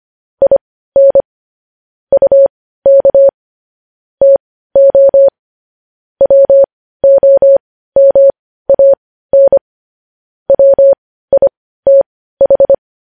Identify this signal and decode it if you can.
Morse code, heard as a keyed tone, about 25 wpm, spaced as for 15 wpm, 550 Hz. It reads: IN UK TO WOMAN WITH
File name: podcast / News headlines-1x-25x15